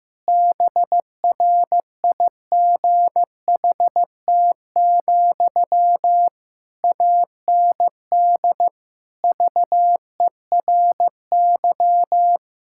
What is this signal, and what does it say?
Morse code, heard as a keyed tone, about 15 wpm, 700 Hz. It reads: BRIGHT, AND VERY
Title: Sentences from Top 1600 Words 15wpm